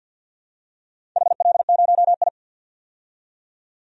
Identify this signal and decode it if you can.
Morse code, heard as a keyed tone, about 52 wpm, 700 Hz. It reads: HL0I
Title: DXCC-100 Call Signs - 4 Characters - 50wpm